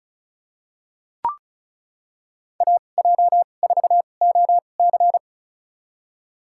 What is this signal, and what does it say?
Morse code, heard as a keyed tone, about 34 wpm, spaced as for 25 wpm, 700 Hz. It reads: AJ4OC
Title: USA Call Signs - 5 Characters - 2x Character Spacing - 35wpm